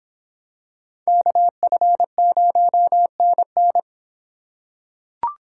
Morse code, heard as a keyed tone, about 26 wpm, 700 Hz. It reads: KF0NN